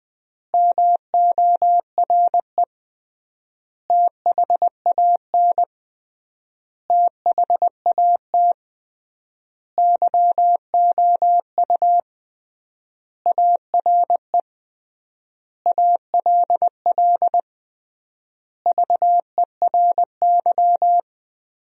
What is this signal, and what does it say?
Morse code, heard as a keyed tone, about 20 wpm, 700 Hz. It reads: MORE THAN THAT YOU ARE ALL VERY
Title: 3x Word-Spacing - Sentences from Top 200 Words - 20wpm